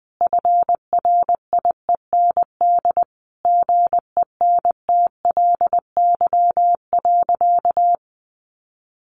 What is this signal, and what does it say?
Morse code, heard as a keyed tone, about 20 wpm, 700 Hz. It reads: FRIEND GENTLY.
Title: Sentences from Top 1500 Words 20wpm